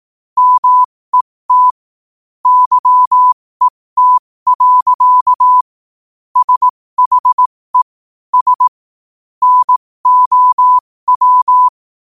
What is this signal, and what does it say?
Morse code, heard as a keyed tone, about 18 wpm, spaced as for 15 wpm, 1000 Hz. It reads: MET YET. SHE S NOW
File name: qotdfiles / 15wpm